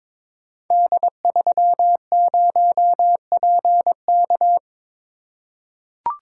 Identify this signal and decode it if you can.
Morse code, heard as a keyed tone, about 22 wpm, 700 Hz. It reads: D30PK
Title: Code Groups - 5 Letter:Numbers - 22wpm